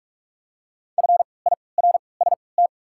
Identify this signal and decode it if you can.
Morse code, heard as a keyed tone, about 44 wpm, spaced as for 25 wpm, 690 Hz. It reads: FIRST
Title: 3x Character Spacing - Top 100 Words - 45wpm